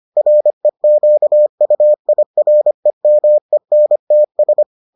Morse code, heard as a keyed tone, about 25 wpm, 600 Hz. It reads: REQUIREMENTS